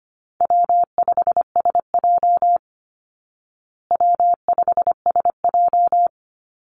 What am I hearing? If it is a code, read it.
Morse code, heard as a keyed tone, about 25 wpm, 700 Hz. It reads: W5SJ W5SJ
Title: Easy Call Signs Repeated 3x - 25wpm